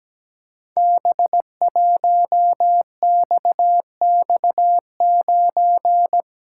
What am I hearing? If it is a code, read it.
Morse code, heard as a keyed tone, about 17 wpm, 700 Hz. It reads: B1XX9
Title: Code Groups - 5 Letter:Numbers - 17wpm